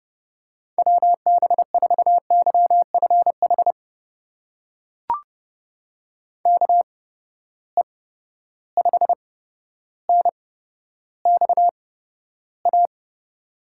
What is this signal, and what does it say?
Morse code, heard as a keyed tone, about 30 wpm, spaced as for 13 wpm, 700 Hz. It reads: WB4YFH KE5NXA
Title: USA Call Signs - 6 Characters - 8x Character Spacing - 30wpm